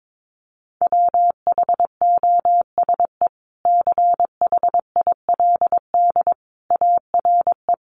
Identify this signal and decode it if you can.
Morse code, heard as a keyed tone, about 22 wpm, 700 Hz. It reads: WHOSE CHILD ARE
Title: Sentences from Top 1200 Words 22wpm